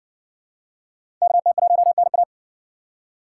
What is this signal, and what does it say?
Morse code, heard as a keyed tone, about 63 wpm, 700 Hz. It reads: KT1NA